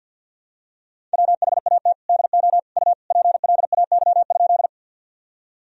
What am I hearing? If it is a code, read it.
Morse code, heard as a keyed tone, about 50 wpm, 700 Hz. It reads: WHAT DO U PLAY?
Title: Sentences from Top 200 Words and Abbreviations 50wpm